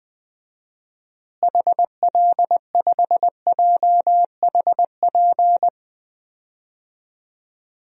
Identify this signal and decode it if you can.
Morse code, heard as a keyed tone, about 20 wpm, 700 Hz. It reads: HL5JHP
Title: DXCC-100 Call Signs - 6 Characters - Repeated 2x - 20wpm